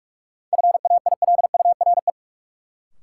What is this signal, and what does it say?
Morse code, heard as a keyed tone, about 43 wpm, 700 Hz. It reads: FAILURE
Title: Sentences from Top 900 Words - New Word Drill - 45wpm